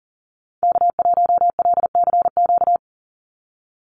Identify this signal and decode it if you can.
Morse code, heard as a keyed tone, about 37 wpm, 690 Hz. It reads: K1LCQ
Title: USA Call Signs 40wpm